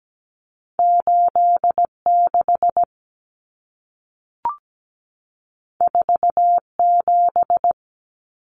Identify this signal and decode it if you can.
Morse code, heard as a keyed tone, about 17 wpm, 700 Hz. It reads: 86 47
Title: Numbers - 2-Digit 17wpm